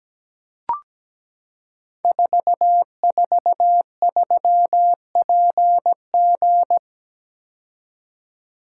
Morse code, heard as a keyed tone, about 17 wpm, 700 Hz. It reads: 443PG
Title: Code Groups - 5 Letter:Numbers - 17wpm